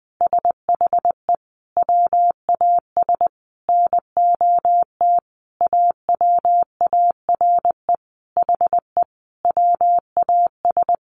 Morse code, heard as a keyed tone, about 20 wpm, 700 Hz. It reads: SHE WAS NOT AWARE HE WAS
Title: Sentences from Top 1200 Words 20wpm